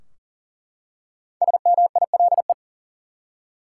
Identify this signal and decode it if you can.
Morse code, heard as a keyed tone, about 40 wpm, 700 Hz. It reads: SMILE
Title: Five Letter Words - 40wpm